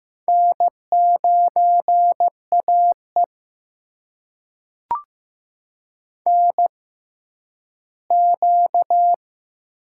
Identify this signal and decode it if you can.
Morse code, heard as a keyed tone, about 15 wpm, spaced as for 10 wpm, 700 Hz. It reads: N9AE NQ
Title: USA Call Signs - 4 Characters - 6x Character Spacing - 15wpm